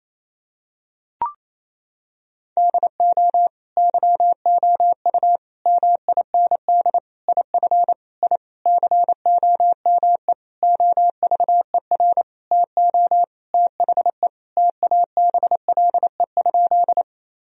Morse code, heard as a keyed tone, about 28 wpm, 700 Hz. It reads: DO YOU MIND IF I COME OVER TO THE TABLE?
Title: Sentences from Top 1000 Words - Review Part 2 - 28wpm